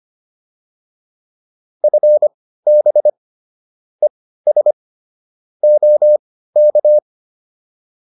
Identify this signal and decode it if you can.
Morse code, heard as a keyed tone, about 25 wpm, spaced as for 15 wpm, 600 Hz. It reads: FB ES OK